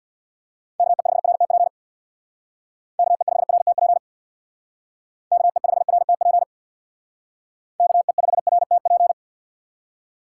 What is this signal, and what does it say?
Morse code, heard as a keyed tone, about 62 wpm, 690 Hz. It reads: KE5RTP KE5RTP KE5RTP KE5RTP
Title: Speed Racing - USA 2x3 Call Signs 50wpm